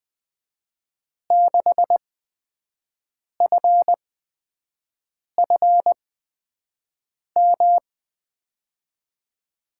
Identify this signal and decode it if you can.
Morse code, heard as a keyed tone, about 21 wpm, spaced as for 6 wpm, 700 Hz. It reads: 6FFM
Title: USA Call Signs - 8x Character Spacing - 20wpm